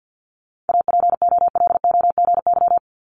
Morse code, heard as a keyed tone, about 50 wpm, 690 Hz. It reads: APOLOGY
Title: Multi-letter Phonemes Repeated 3x 50wpm